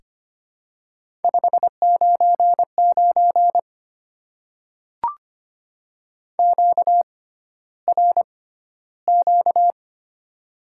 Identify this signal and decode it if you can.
Morse code, heard as a keyed tone, about 25 wpm, spaced as for 13 wpm, 700 Hz. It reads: 599 QRQ
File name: Top 90 QSO Elements - 6x Character Spacing - 25wpm